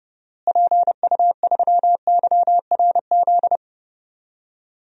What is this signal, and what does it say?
Morse code, heard as a keyed tone, about 30 wpm, 700 Hz. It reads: PU3YRZ
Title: DXCC-100 Call Signs - 6 Characters - Repeated 3x - 30wpm